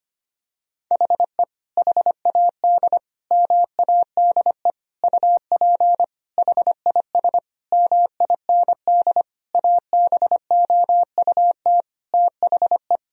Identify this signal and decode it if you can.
Morse code, heard as a keyed tone, about 25 wpm, 700 Hz. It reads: HE HAD MADE UP HIS MIND ABOUT THE
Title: Sentences from Top 1000 Words - Review Part 3 - 25wpm